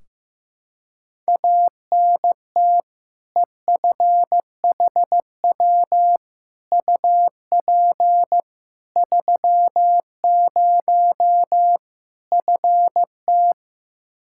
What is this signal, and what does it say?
Morse code, heard as a keyed tone, about 15 wpm, 700 Hz. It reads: ANT EFHW UP 30 FT
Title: Practice QSO - Exchange 2 - Info - 15wpm